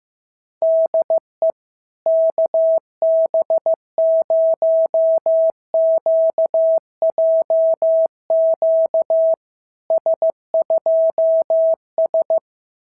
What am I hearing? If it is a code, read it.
Morse code, heard as a keyed tone, about 15 wpm, 650 Hz. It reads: DE KB0QJQ S2S
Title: SOTA QSO Practice 15wpm